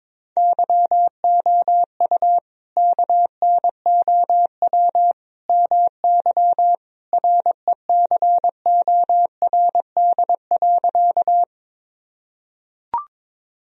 Morse code, heard as a keyed tone, about 22 wpm, 700 Hz. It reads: YOU KNOW MY RECORD.
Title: Sentences from Top 1000 Words - Review Part 3 - 22wpm